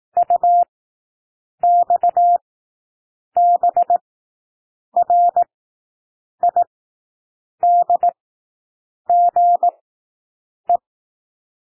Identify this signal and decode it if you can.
Morse code, heard as a keyed tone, about 18 wpm, spaced as for 7 wpm, 700 Hz. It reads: UXBRIDGE
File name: SOUND WORDS 18 wpm